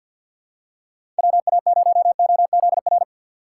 Morse code, heard as a keyed tone, about 48 wpm, 700 Hz. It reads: WA0OZR
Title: USA Call Signs - 6 Characters - 6x Character Spacing - 50wpm